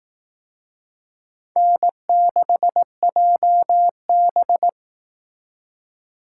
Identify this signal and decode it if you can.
Morse code, heard as a keyed tone, about 19 wpm, 700 Hz. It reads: N6JB